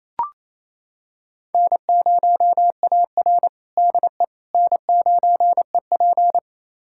Morse code, heard as a keyed tone, about 27 wpm, 700 Hz. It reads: N0AR DE N9EP
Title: USA Callsign DE Callsign - 4 Characters - 28wpm